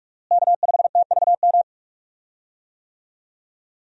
Morse code, heard as a keyed tone, about 46 wpm, 700 Hz. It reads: K5TVM
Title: USA Call Signs - 5 Characters - Repeated 2x - 45wpm